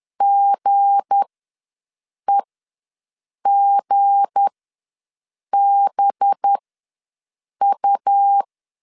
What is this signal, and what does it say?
Morse code, heard as a keyed tone, about 10 wpm, spaced as for 6 wpm, 800 Hz. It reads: GEGBU
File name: LEZIONI / 7 GF9Z / 7 G ws 10wpm 3843